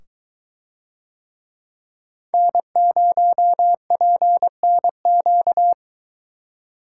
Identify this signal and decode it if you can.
Morse code, heard as a keyed tone, about 22 wpm, 700 Hz. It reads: N0PNQ